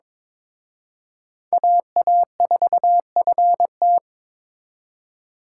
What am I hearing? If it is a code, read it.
Morse code, heard as a keyed tone, about 26 wpm, 700 Hz. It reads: AA4FT
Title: Speed Racing - USA 2x2 Call Signs 22wpm